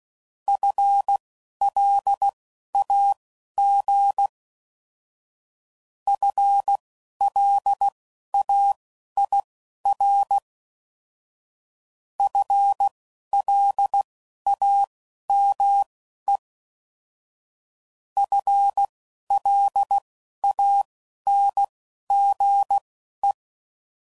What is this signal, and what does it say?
Morse code, heard as a keyed tone, about 16 wpm, spaced as for 11 wpm, 800 Hz. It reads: FLAG FLAIR FLAME FLANGE
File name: Prefixes / FLs & THs / FL Runs